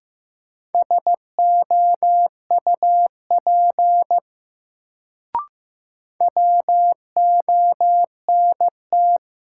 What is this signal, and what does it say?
Morse code, heard as a keyed tone, about 15 wpm, 700 Hz. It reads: SOUP WONT
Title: Sentences from Top 1000 Words - Review Part 6 - New Word Drill - 15wpm